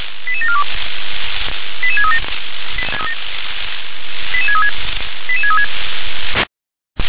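Five-tone selective call (ZVEI), heard as quick runs of five tones, tones of about 70 ms. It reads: 88742 88637 88636 88636
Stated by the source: ZVEI five-tone selective call 88730 received via SDR.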